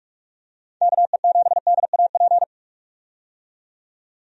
Morse code, heard as a keyed tone, about 46 wpm, 700 Hz. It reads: KE7DAP
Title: USA Call Signs - 6 Characters - 45wpm